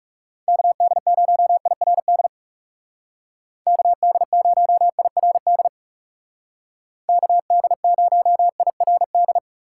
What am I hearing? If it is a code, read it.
Morse code, heard as a keyed tone, about 40 wpm, 700 Hz. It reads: KD0IRD KD0IRD KD0IRD